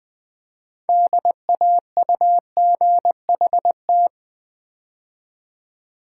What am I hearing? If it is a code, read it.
Morse code, heard as a keyed tone, about 21 wpm, 700 Hz. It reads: DAUGHT
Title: Build A Word - 2 Letters At A Time - 8-Letter Words - 20wpm